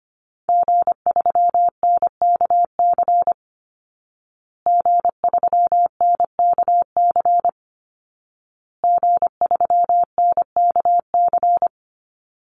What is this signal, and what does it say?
Morse code, heard as a keyed tone, about 25 wpm, 700 Hz. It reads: G3NKC G3NKC G3NKC
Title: Call Signs Repeated 2x - 25wpm